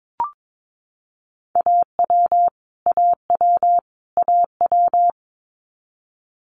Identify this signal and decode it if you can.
Morse code, heard as a keyed tone, about 22 wpm, 700 Hz. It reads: AW AW AW
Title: Multi-letter Phonemes Repeated 3x 22wpm